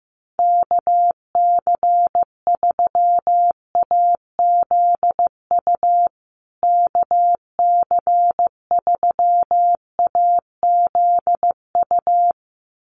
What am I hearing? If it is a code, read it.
Morse code, heard as a keyed tone, about 14 wpm, 700 Hz. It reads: KC3AZU KC3AZU
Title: USA Call Signs Repeated 2x 15wpm